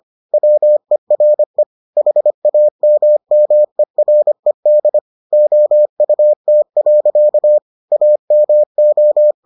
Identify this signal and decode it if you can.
Morse code, heard as a keyed tone, about 25 wpm, 600 Hz. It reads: WERE HAMMERED OUT. AMO